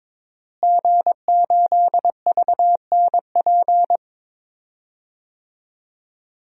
Morse code, heard as a keyed tone, about 22 wpm, 700 Hz. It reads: G8VNP
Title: DXCC-100 Call Signs - 5 Characters - Repeated 3x - 22wpm